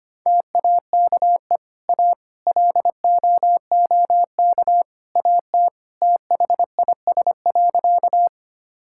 Morse code, heard as a keyed tone, about 25 wpm, 700 Hz. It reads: TAKE A LOOK AT THIS.
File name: practice / 25wpm / Sentences with 3-Word Phrases - 25wpm